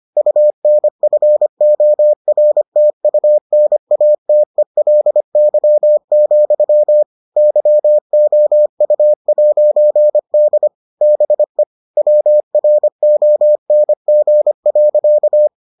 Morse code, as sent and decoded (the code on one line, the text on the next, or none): ..- -. ..-. --- .-. - ..- -. .- - . .-.. -.-- --..-- / -.-- --- ..- .----. -.. / -... / .-- .-. --- -. --. .-.-.-
UNFORTUNATELY, YOU'D BE WRONG.